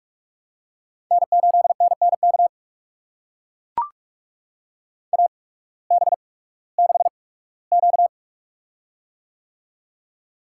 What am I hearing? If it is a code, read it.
Morse code, heard as a keyed tone, about 44 wpm, spaced as for 21 wpm, 700 Hz. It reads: N8NNK AB6Q
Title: USA Call Signs - 8x Character Spacing - 45wpm